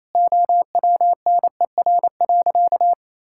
Morse code, heard as a keyed tone, about 28 wpm, 700 Hz. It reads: OWNER.
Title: Sentences from Top 1600 Words 28wpm